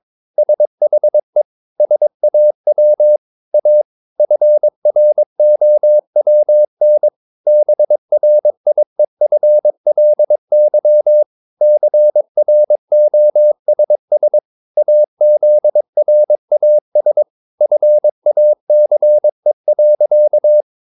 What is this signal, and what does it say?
Morse code, heard as a keyed tone, about 22 wpm, 600 Hz. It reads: SHE SAW A FROWN BRIEFLY CROSS AZRAS FACE.
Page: Romanen Never af Ken Follet på engelsk.